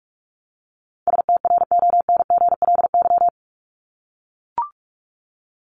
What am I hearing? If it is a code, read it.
Morse code, heard as a keyed tone, about 44 wpm, 700 Hz. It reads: STRONGLY
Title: LY Suffix - Top 100 Words with Base Words 45wpm